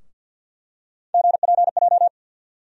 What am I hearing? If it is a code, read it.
Morse code, heard as a keyed tone, about 50 wpm, 700 Hz. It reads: GPJ
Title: USA Call Sign Postfixes - 3 Letters 50wpm